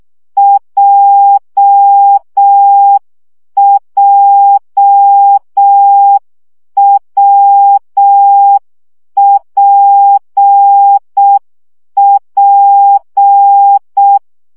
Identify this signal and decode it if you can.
Morse code, heard as a keyed tone, about 6 wpm, 800 Hz. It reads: JJWPP
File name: CW gyakorlas w p j 5 baud